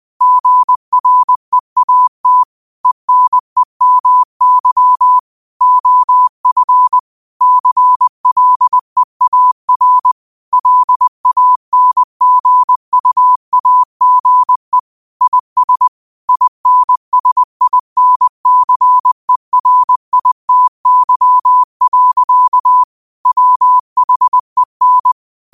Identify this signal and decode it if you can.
Morse code, heard as a keyed tone, about 20 wpm, 1000 Hz. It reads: GREAT ENEMY OF CLEAR LANGUAGE IS INSINCERITY. WHEN
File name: qotdfiles / 20wpm